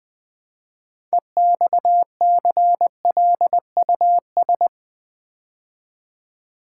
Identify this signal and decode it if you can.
Morse code, heard as a keyed tone, about 20 wpm, 700 Hz. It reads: EXCLUS
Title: Build A Word - 3 Letters At A Time - 9-Letter Words - 20wpm